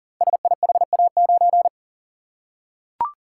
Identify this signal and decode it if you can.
Morse code, heard as a keyed tone, about 39 wpm, 690 Hz. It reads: SIHA9
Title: Code Groups - 5 Letter:Numbers - 40wpm